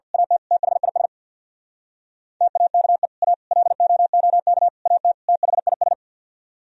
Morse code, heard as a keyed tone, about 51 wpm, 700 Hz. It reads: AT THIS TAKE A LOOK AT THIS
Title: Speed-Racing - Sets of 5 Words 50wpm